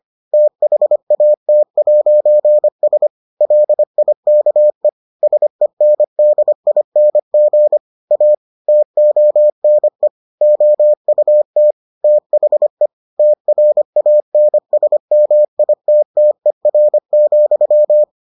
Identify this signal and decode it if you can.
Morse code, heard as a keyed tone, about 25 wpm, 600 Hz. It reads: THAT'S LIKE SENDING A TONE OUT THE TRANSMITTER,